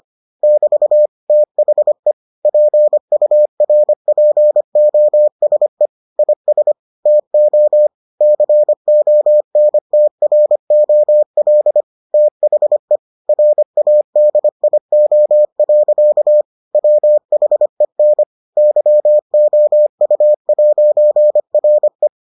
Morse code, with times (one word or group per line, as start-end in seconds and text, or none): 0.4-1.1 <BT>
1.3-2.1 THE
2.5-5.9 PURPOSE
6.2-6.7 IS
7.1-7.9 TO
8.2-11.8 CONTROL
12.1-13.0 THE
13.3-16.4 RADIO.
16.7-18.2 WHEN
18.6-22.1 YOU'RE